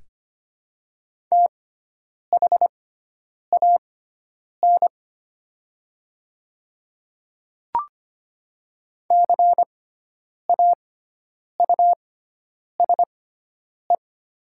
Morse code, heard as a keyed tone, about 25 wpm, spaced as for 9 wpm, 700 Hz. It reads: THAN CAUSE
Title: 6x Character Spacing - Top 200 Words - 25wpm